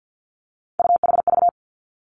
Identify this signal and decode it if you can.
Morse code, heard as a keyed tone, about 50 wpm, 700 Hz. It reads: UHV
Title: Three Challenging Letter-Number Combinations 50wpm